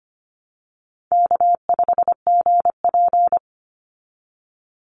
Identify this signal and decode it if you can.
Morse code, heard as a keyed tone, about 25 wpm, 700 Hz. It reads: K5GP
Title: Call Signs Repeated 2x - 25wpm